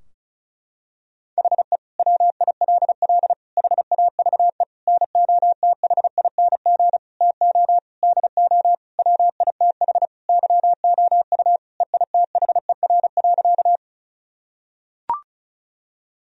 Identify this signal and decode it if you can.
Morse code, heard as a keyed tone, about 34 wpm, 700 Hz. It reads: HE WILL HAVE NOTHING TO DO WITH YOU EITHER.
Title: Sentences from Top 1000 Words - Review Part 4 - 35wpm